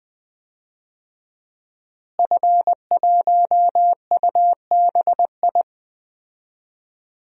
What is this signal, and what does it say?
Morse code, heard as a keyed tone, about 20 wpm, 700 Hz. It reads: F1UBI
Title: DXCC-100 Call Signs - 5 Characters - Repeated 3x - 20wpm